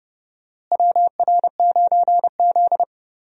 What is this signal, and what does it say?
Morse code, heard as a keyed tone, about 30 wpm, 700 Hz. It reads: WR9Z
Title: USA Call Signs - 4 Characters - Repeated 2x - 30wpm